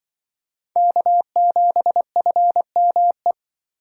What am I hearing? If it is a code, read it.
Morse code, heard as a keyed tone, about 24 wpm, 700 Hz. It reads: K7FME
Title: Speed Racing - USA Call Signs 20wpm